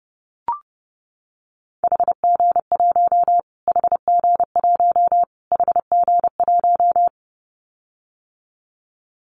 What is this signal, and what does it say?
Morse code, heard as a keyed tone, about 30 wpm, 700 Hz. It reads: HG1 HG1 HG1
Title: Callsign Prefixes for Top 25 DXCC - Repeated 3x - 30wpm